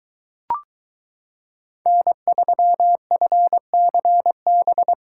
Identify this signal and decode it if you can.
Morse code, heard as a keyed tone, about 22 wpm, 700 Hz. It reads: N3FCB